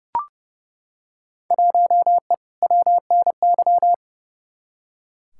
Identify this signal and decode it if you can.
Morse code, heard as a keyed tone, about 32 wpm, 700 Hz. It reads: 1E WNY
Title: Field Day Exchanges - 30wpm